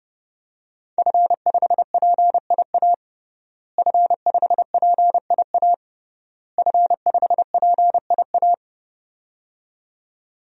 Morse code, heard as a keyed tone, about 30 wpm, 700 Hz. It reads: F5PIA F5PIA F5PIA
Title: DXCC-100 Call Signs - 5 Characters - Repeated 3x - 30wpm